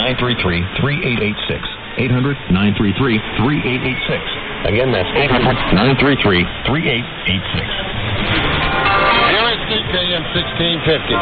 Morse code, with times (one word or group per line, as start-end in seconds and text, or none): none